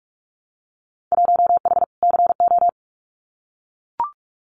Morse code, heard as a keyed tone, about 44 wpm, 700 Hz. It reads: 1H CO
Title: Winter Field Day Exchanges 45wpm